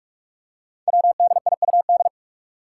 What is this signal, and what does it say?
Morse code, heard as a keyed tone, about 44 wpm, 700 Hz. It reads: WDIUD
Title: Code Groups - 5 Letter:Numbers - 45wpm